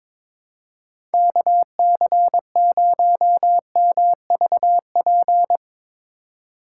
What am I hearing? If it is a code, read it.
Morse code, heard as a keyed tone, about 25 wpm, 700 Hz. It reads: KC0MVP
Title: Speed Racing - USA Call Signs 22wpm